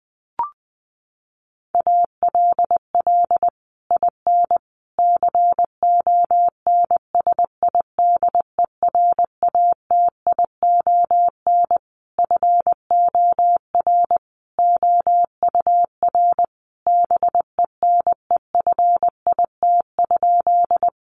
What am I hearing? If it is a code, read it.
Morse code, heard as a keyed tone, about 20 wpm, 700 Hz. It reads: ALL IN CONSIDERATION FOR OUR BENEFIT?
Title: Sentences from Top 1500 Words 20wpm